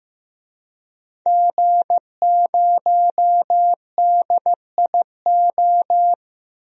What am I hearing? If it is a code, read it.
Morse code, heard as a keyed tone, about 15 wpm, 700 Hz. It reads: G0DIO
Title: DXCC-100 Call Signs - 5 Characters - 15wpm